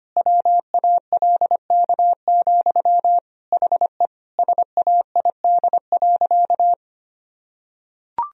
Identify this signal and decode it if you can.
Morse code, heard as a keyed tone, about 25 wpm, 700 Hz. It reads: WALK, HE SAID.
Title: Sentences from Top 1000 Words - Review Part 1 - 25wpm